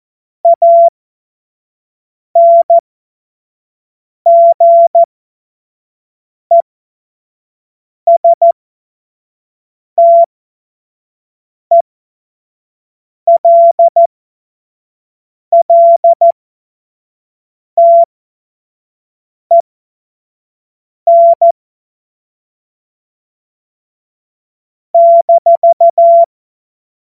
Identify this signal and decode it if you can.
Morse code, heard as a keyed tone, about 14 wpm, spaced as for 5 wpm, 680 Hz. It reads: ANGESTELLTEN -
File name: Morsekurs / Witze / Witz 18. Juli 2023 25 F70